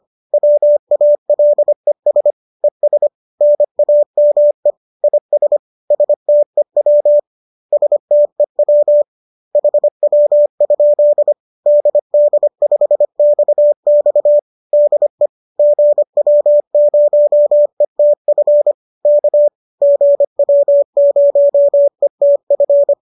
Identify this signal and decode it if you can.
Morse code, heard as a keyed tone, about 25 wpm, 590 Hz. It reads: WALES ES NAME IS STEW STEW HW? DD5XX DE GW0ETF K GW0ETF
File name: qso218 25